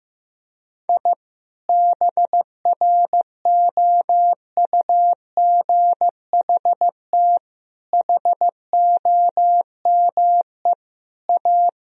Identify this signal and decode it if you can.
Morse code, heard as a keyed tone, about 15 wpm, 700 Hz. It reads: I BROUGHT HOME A
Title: Sentences from Top 1000 Words - Review Part 4 - 15wpm